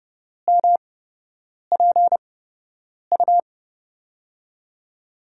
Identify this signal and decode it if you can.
Morse code, heard as a keyed tone, about 31 wpm, spaced as for 8 wpm, 700 Hz. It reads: MPU
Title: USA Call Signs - 5 Characters - 8x Character Spacing - 30wpm